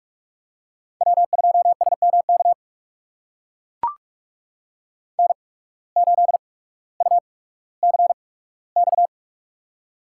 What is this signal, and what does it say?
Morse code, heard as a keyed tone, about 44 wpm, spaced as for 19 wpm, 700 Hz. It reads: W2SMK N8UCX
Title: USA Call Signs - 5 Characters - 8x Character Spacing - 45wpm